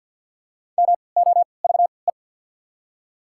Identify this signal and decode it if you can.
Morse code, heard as a keyed tone, about 53 wpm, spaced as for 29 wpm, 700 Hz. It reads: MOVE
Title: 3x Character Spacing - Top 200 Words - 50wpm